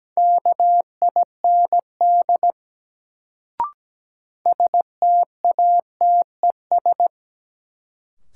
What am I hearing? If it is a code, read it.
Morse code, heard as a keyed tone, about 17 wpm, 700 Hz. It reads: KIND STATES
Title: Sets of 2 Words - New Word Drill - 17wpm